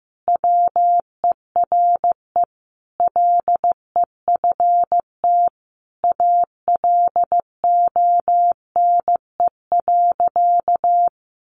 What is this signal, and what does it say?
Morse code, heard as a keyed tone, about 15 wpm, 700 Hz. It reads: WERE LEFT ALONE.
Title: Sentences from Top 1000 Words 15wpm